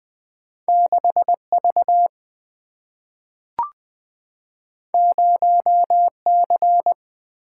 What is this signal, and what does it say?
Morse code, heard as a keyed tone, about 20 wpm, 700 Hz. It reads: 6 V 0 C
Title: Single-Letter Single-Number 20wpm